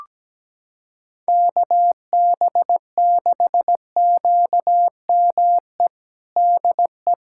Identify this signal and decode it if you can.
Morse code, heard as a keyed tone, about 17 wpm, 700 Hz. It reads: KB6QME DE
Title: USA Callsign DE Callsign - 6 Characters - 17wpm